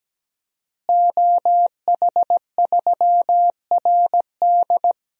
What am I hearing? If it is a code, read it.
Morse code, heard as a keyed tone, about 17 wpm, 700 Hz. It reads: OH3RD
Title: DXCC-100 Call Signs - 5 Characters - Repeated 3x - 17wpm